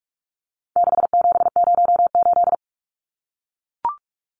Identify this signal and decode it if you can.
Morse code, heard as a keyed tone, about 44 wpm, 700 Hz. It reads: 6708
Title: Numbers - 4-Digit 45wpm